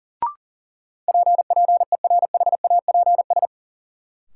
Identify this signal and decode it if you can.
Morse code, heard as a keyed tone, about 40 wpm, 700 Hz. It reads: P PERHAPS
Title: Sentences from Top 1100 Words - New Word Drill Alt - 40wpm